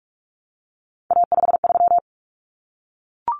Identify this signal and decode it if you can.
Morse code, heard as a keyed tone, about 44 wpm, 700 Hz. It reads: A53
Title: Common Cut Numbers - 3-Digit 45wpm